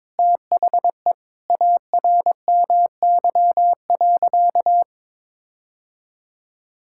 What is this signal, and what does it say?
Morse code, heard as a keyed tone, about 22 wpm, 700 Hz. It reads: THE ARMY.
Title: Sentences from Top 1000 Words - Review Part 5 - 22wpm